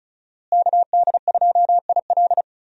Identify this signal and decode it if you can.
Morse code, heard as a keyed tone, about 36 wpm, 700 Hz. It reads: KD2IL
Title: USA Call Signs - 5 Characters - Repeated 2x - 35wpm